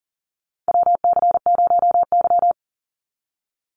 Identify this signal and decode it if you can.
Morse code, heard as a keyed tone, about 39 wpm, 700 Hz. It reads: WC0Y
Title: NA SOTA Chaser Callsigns 40wpm